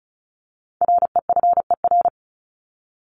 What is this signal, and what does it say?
Morse code, heard as a keyed tone, about 34 wpm, 700 Hz. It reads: REFER
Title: ED Suffix - Top 100 Words with Base Words 35wpm